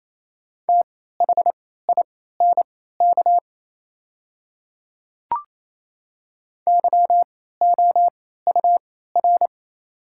Morse code, heard as a keyed tone, about 28 wpm, spaced as for 16 wpm, 700 Hz. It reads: THINK YOUR
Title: 3x Character Spacing - Top 200 Words - 28wpm